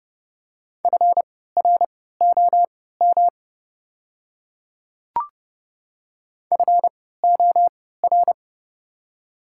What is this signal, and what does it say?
Morse code, heard as a keyed tone, about 30 wpm, spaced as for 17 wpm, 700 Hz. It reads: FROM FOR
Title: 3x Character Spacing - Top 100 Words - 30wpm